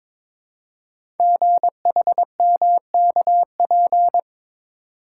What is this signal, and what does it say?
Morse code, heard as a keyed tone, about 22 wpm, 700 Hz. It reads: GHMKP
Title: Code Groups - 5 Letters - 22wpm